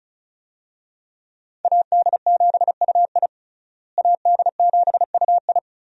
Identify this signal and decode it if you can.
Morse code, heard as a keyed tone, about 36 wpm, 700 Hz. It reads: AD7UI AD7UI
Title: USA Call Signs - 5 Characters - Repeated 3x - 35wpm